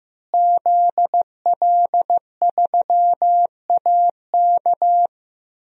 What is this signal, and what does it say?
Morse code, heard as a keyed tone, about 15 wpm, 700 Hz. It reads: ZL3AK
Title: DXCC-100 Call Signs - 5 Characters - Repeated 2x - 15wpm